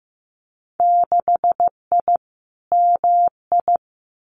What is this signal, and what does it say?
Morse code, heard as a keyed tone, about 15 wpm, 700 Hz. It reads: 6I MI
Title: Winter Field Day Exchanges Encore 15wpm